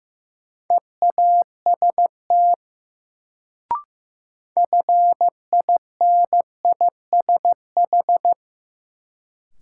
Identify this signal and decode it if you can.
Morse code, heard as a keyed tone, about 15 wpm, 700 Hz. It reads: EAST FINISH
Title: Sentences from Top 1000 Words - Review Part 3 - New Word Drill - 15wpm